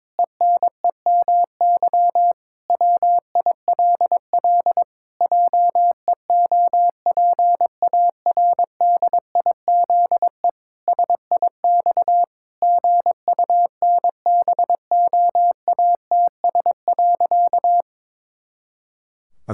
Morse code, heard as a keyed tone, about 22 wpm, 700 Hz. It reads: ENEMY WILL JEOPARDIZE SIX GUNBOATS.
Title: Pangram Challenge v2 - 22wpm